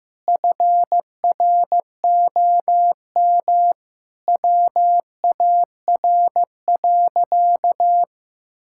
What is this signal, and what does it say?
Morse code, heard as a keyed tone, about 15 wpm, 700 Hz. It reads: FROM WAR.
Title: Sentences from Top 1000 Words - Review Part 6 - 15wpm